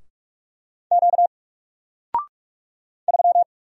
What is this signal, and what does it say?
Morse code, heard as a keyed tone, about 45 wpm, 700 Hz. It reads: Q 3
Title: Single Letter-Number Alt 45wpm